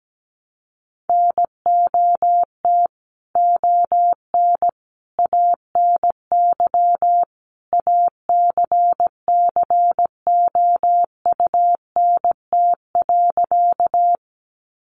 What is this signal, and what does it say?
Morse code, heard as a keyed tone, about 17 wpm, 700 Hz. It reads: NOT ON ANY ACCOUNT.
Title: Sentences from Top 900 Words 17wpm